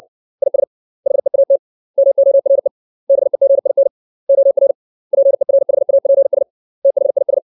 Morse code, heard as a keyed tone, about 60 wpm, 550 Hz. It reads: IS HEAT MORE BEGIN OR PERHAPS THIS
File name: 632 words / 632 Words 60WPM